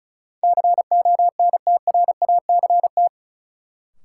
Morse code, heard as a keyed tone, about 36 wpm, 700 Hz. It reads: CONTRACT
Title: Sentences from Top 1400 Words - New Word Drill - 35wpm